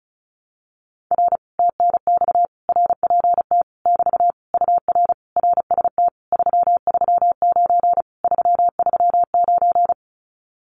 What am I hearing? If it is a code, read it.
Morse code, heard as a keyed tone, about 36 wpm, 700 Hz. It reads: R TNX RPT <BT> UR RST 339 339
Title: Common QSO Phrases Encore 35wpm